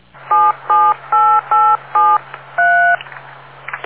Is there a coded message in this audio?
Yes: DTMF from 0.3 to 3.0 s, heard as telephone touch tones, tones of about 250 ms. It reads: 778873